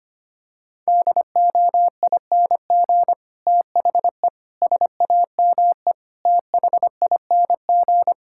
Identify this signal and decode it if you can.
Morse code, heard as a keyed tone, about 25 wpm, 700 Hz. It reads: DOING THE SAME THING
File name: Sets of 9 Words - 25wpm